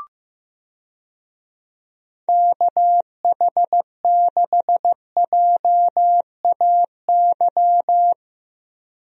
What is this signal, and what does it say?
Morse code, heard as a keyed tone, about 15 wpm, 700 Hz. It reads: KH6JAY
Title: DXCC-100 Call Signs - 6 Characters - 15wpm